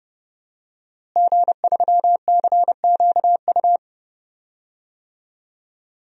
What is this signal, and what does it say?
Morse code, heard as a keyed tone, about 31 wpm, 700 Hz. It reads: G3CQU
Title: DXCC-100 Call Signs - 5 Characters - Repeated 2x - 30wpm